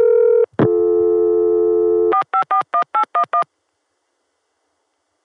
DTMF, heard as telephone touch tones, tones of about 100 ms. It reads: 4541811